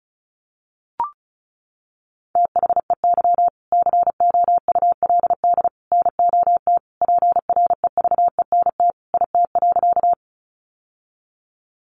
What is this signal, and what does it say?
Morse code, heard as a keyed tone, about 34 wpm, 700 Hz. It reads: THEY COULD NOT PREVENT IT.
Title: Sentences from Top 1100 Words 35wpm